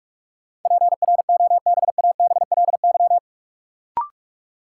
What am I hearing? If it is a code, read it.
Morse code, heard as a keyed tone, about 46 wpm, 700 Hz. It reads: PROBABLY
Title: Sentences from Top 700 Words - New Word Drill - 45wpm